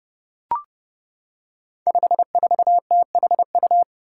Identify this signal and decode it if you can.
Morse code, heard as a keyed tone, about 30 wpm, 700 Hz. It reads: 54THU
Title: Five Challenging Letter-Number Combinations 30wpm